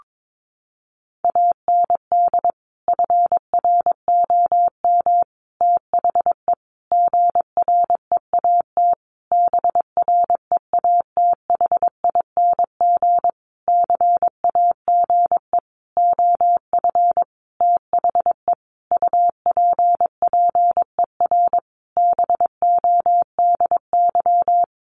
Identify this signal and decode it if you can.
Morse code, heard as a keyed tone, about 22 wpm, 700 Hz. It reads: AND FROM THE GREAT BREATHING CAGE OF THE UPPER BODY
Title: A Sound Of Thunder 22wpm